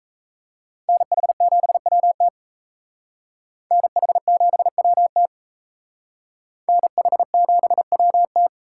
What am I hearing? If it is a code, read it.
Morse code, heard as a keyed tone, about 35 wpm, 700 Hz. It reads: NH7WT NH7WT NH7WT